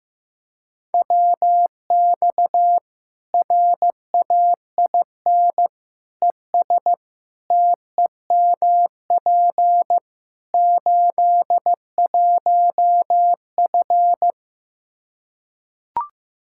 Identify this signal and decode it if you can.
Morse code, heard as a keyed tone, about 15 wpm, 700 Hz. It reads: WX RAIN ES TEMP 81F
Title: Practice QSO - Exchange 2 - Info - 15wpm